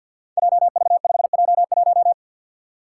Morse code, heard as a keyed tone, about 47 wpm, 700 Hz. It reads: JV5J1
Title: Five Challenging Letter-Number Combinations 50wpm